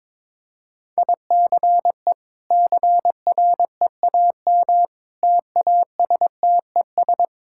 Morse code, heard as a keyed tone, about 22 wpm, 700 Hz. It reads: ICE CREAM TASTES
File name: practice / 22wpm / Easy Sentences with Hints - 22wpm